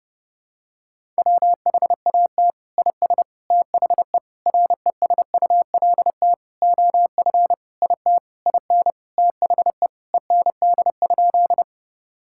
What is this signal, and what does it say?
Morse code, heard as a keyed tone, about 30 wpm, 700 Hz. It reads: WHAT IS THE RESULT OF IT IN THE END?
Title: Sentences from Top 1000 Words - Review Part 5 - 30wpm